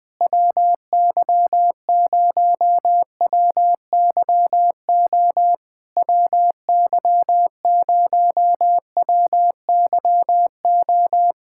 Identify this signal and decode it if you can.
Morse code, heard as a keyed tone, about 20 wpm, 700 Hz. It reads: WY0WYO WY0WYO
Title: SOTA QSO Practice 20wpm